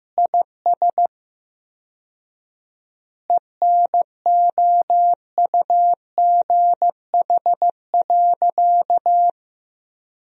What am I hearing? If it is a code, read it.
Morse code, heard as a keyed tone, about 16 wpm, 700 Hz. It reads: IS ENOUGH.
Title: 4x Word-Spacing - Sentences from Top 400 Words - 15wpm